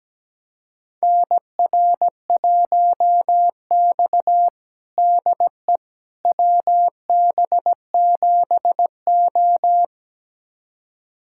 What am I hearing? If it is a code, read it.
Morse code, heard as a keyed tone, about 17 wpm, 700 Hz. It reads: NR1X DE WB7O
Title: USA Callsign DE Callsign - 4 Characters - 17wpm